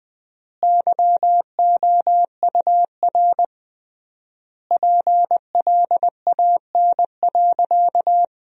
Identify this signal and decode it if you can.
Morse code, heard as a keyed tone, about 20 wpm, 700 Hz. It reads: YOUR PLAN.
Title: 3x Word-Spacing - Sentences from Top 500 Words - 20wpm